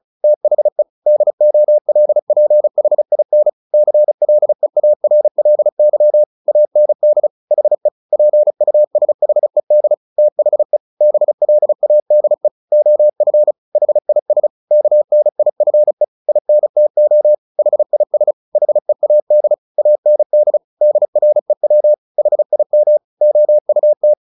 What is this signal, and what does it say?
Morse code, heard as a keyed tone, about 34 wpm, 600 Hz. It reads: THE DOLPHIN CLEARLY AND HE PUSHED THE BLADE OF HIS KNIFE INTO HIS HEAD AND DREW HIM OUT
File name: the old man and the sea 2 35wpm